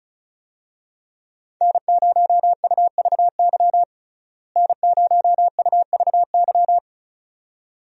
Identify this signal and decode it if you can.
Morse code, heard as a keyed tone, about 36 wpm, 700 Hz. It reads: N0UVY N0UVY
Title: USA Call Signs - 5 Characters - Repeated 2x - 35wpm